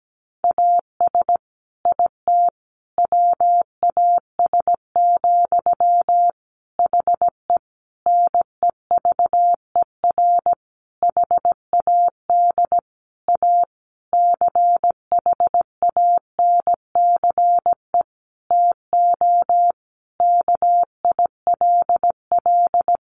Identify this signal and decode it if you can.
Morse code, heard as a keyed tone, about 17 wpm, 700 Hz. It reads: AS IT WAS, HE NEVER HAD A CHANCE TO KILL
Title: Sentences from Top 1400 Words 17wpm